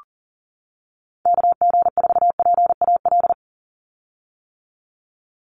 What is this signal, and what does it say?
Morse code, heard as a keyed tone, about 40 wpm, 700 Hz. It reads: KG4PAL
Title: USA Call Signs 40wpm